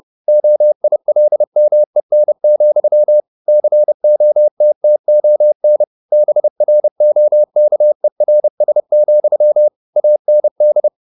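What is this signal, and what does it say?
Morse code, heard as a keyed tone, about 30 wpm, 600 Hz. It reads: OILMEN, COTTON BROKERS, AND
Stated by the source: Morse code at 30 wpm, the novel Never by Ken Follett in English, from CW med Gnister.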